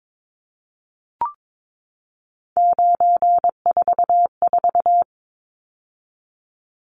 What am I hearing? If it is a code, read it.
Morse code, heard as a keyed tone, about 22 wpm, 700 Hz. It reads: 944
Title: Numbers - 3-Digit 22wpm